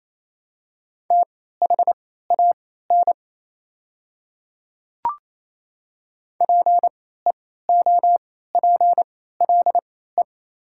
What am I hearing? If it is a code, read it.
Morse code, heard as a keyed tone, about 28 wpm, spaced as for 16 wpm, 700 Hz. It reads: THAN PEOPLE
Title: 3x Character Spacing - Top 100 Words - 28wpm